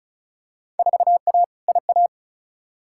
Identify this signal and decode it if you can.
Morse code, heard as a keyed tone, about 34 wpm, 700 Hz. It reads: 4A IA
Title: Field Day Exchanges - 35wpm